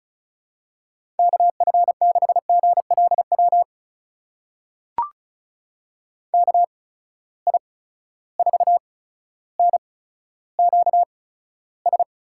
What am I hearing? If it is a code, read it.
Morse code, heard as a keyed tone, about 36 wpm, spaced as for 15 wpm, 700 Hz. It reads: KF6GLW KI4NQS
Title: USA Call Signs - 6 Characters - 8x Character Spacing - 35wpm